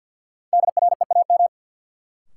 Morse code, heard as a keyed tone, about 52 wpm, 700 Hz. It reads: DREAM
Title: Sentences from Top 1500 Words - New Word Drill - 50wpm